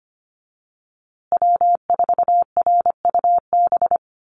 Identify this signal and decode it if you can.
Morse code, heard as a keyed tone, about 25 wpm, 700 Hz. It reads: W4RUB
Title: USA Call Signs Repeated 2x 25wpm